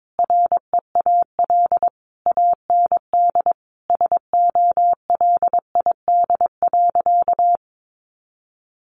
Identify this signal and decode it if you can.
Morse code, heard as a keyed tone, about 22 wpm, 700 Hz. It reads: REAL AND SOLID.
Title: Sentences from Top 1500 Words 22wpm